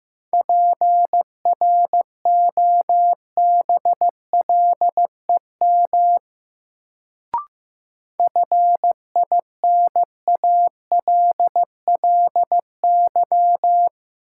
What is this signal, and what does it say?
Morse code, heard as a keyed tone, about 15 wpm, 700 Hz. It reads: PROBLEM FINALLY
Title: Sentences from Top 1000 Words - New Word Drill - 15wpm